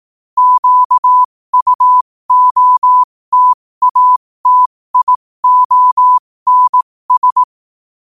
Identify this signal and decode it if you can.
Morse code, heard as a keyed tone, about 18 wpm, spaced as for 15 wpm, 1000 Hz. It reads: QUOTATIONS
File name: qotdfiles / 15wpm